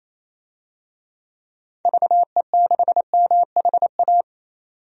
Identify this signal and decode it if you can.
Morse code, heard as a keyed tone, about 28 wpm, 700 Hz. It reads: VE6MHA
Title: DXCC-100 Call Signs - 6 Characters - 28wpm